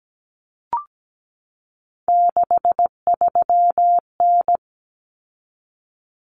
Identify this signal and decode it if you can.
Morse code, heard as a keyed tone, about 17 wpm, 700 Hz. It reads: 63N
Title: Common Cut Numbers - 3-Digit 17wpm